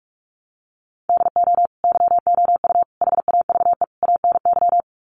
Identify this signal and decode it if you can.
Morse code, heard as a keyed tone, about 44 wpm, 700 Hz. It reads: DO YOU HAVE ANY
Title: Sets of 4 Words 45wpm